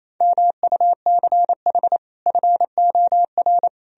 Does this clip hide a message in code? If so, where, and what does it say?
Yes: Morse code from 0.2 to 3.7 s, heard as a keyed tone, about 38 wpm, 700 Hz. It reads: MUCH FOR